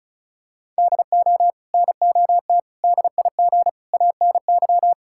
Morse code, heard as a keyed tone, about 34 wpm, 700 Hz. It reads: DO NOT DIG ANY
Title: Sentences from Top 1600 Words 35wpm